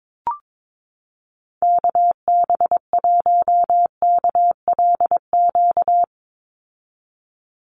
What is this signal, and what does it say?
Morse code, heard as a keyed tone, about 22 wpm, 700 Hz. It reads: KB1KLQ
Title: USA Call Signs 22wpm